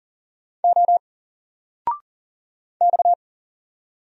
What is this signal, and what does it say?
Morse code, heard as a keyed tone, about 40 wpm, 700 Hz. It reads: O X
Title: Single Character (Letter:Number:Punctuation:Prosign) 40wpm